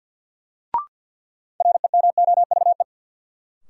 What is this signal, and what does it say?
Morse code, heard as a keyed tone, about 46 wpm, 700 Hz. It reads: REMOVE
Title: Sentences from Top 800 Words - New Word Drill - 50wpm